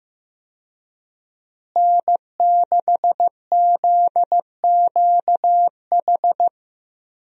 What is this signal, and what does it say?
Morse code, heard as a keyed tone, about 15 wpm, 700 Hz. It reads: N6ZQH
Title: USA Call Signs - 5 Characters - Repeated 2x - 15wpm